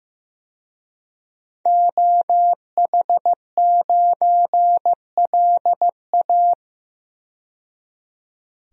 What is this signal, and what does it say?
Morse code, heard as a keyed tone, about 15 wpm, 700 Hz. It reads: OH9LA
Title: DXCC-100 Call Signs - 5 Characters - 15wpm